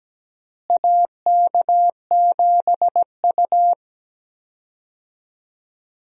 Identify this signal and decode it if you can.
Morse code, heard as a keyed tone, about 17 wpm, 700 Hz. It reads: AK7U
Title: USA Call Signs - 4 Characters - Repeated 2x - 17wpm